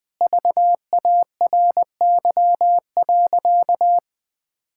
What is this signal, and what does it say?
Morse code, heard as a keyed tone, about 20 wpm, 700 Hz. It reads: VARY.
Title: Sentences with 6-Word Phrases - Spoken Beginning - 20wpm